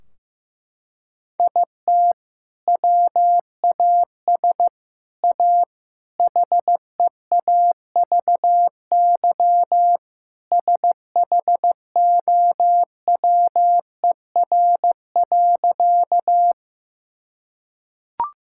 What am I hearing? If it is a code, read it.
Morse code, heard as a keyed tone, about 15 wpm, 700 Hz. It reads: IT WAS A HEAVY SHOWER.
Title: Sentences from Top 1600 Words 15wpm